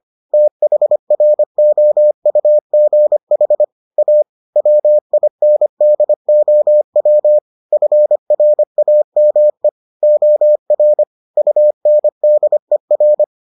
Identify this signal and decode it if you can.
Morse code, heard as a keyed tone, about 25 wpm, 600 Hz. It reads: THROUGH A WINDOW FRAME OR UNDER